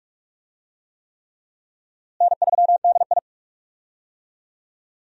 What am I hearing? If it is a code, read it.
Morse code, heard as a keyed tone, about 45 wpm, 700 Hz. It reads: N3DI
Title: USA Call Signs - 4 Characters - Repeated 3x - 45wpm